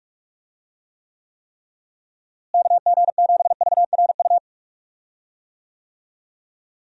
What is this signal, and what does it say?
Morse code, heard as a keyed tone, about 46 wpm, 700 Hz. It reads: KG7VRU
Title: USA Call Signs - 6 Characters - Repeated 3x - 45wpm